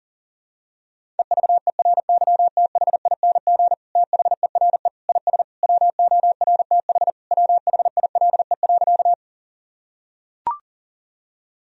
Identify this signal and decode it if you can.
Morse code, heard as a keyed tone, about 40 wpm, 700 Hz. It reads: EVERYTHING THERE IS WORTH WHILE.
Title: Sentences from Top 1000 Words - Review Part 1 - 40wpm